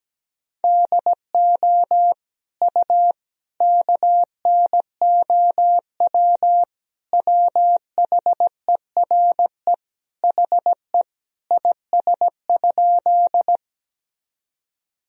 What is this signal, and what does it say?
Morse code, heard as a keyed tone, about 17 wpm, 700 Hz. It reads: DO U KNOW WHERE HE IS?
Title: Sentences from Top 300 Words and Abbreviations 17wpm